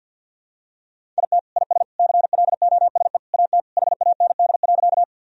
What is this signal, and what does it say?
Morse code, heard as a keyed tone, about 52 wpm, 700 Hz. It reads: IT IS CLOSE AT HAND.
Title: Sentences from Top 1000 Words - Review Part 1 - 50wpm